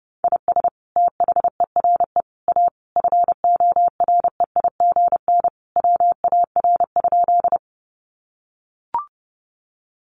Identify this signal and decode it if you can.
Morse code, heard as a keyed tone, about 30 wpm, 700 Hz. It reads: IS THERE A FOREIGN WAR?
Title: Sentences from Top 1400 Words 30wpm